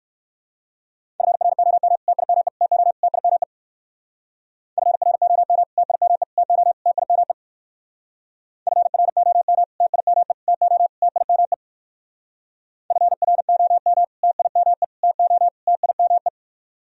Morse code, heard as a keyed tone, about 59 wpm, 700 Hz. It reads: FROM TIME TO TIME FROM TIME TO TIME FROM TIME TO TIME FROM TIME TO TIME